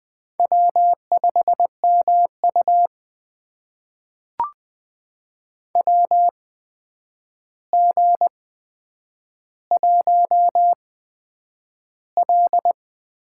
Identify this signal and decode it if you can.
Morse code, heard as a keyed tone, about 20 wpm, spaced as for 9 wpm, 700 Hz. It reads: W5MU WG1L
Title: USA Call Signs - 4 Characters - 8x Character Spacing - 20wpm